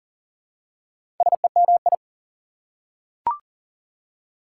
Morse code, heard as a keyed tone, about 40 wpm, 700 Hz. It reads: SEMI
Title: Prefixes and Suffixes 40wpm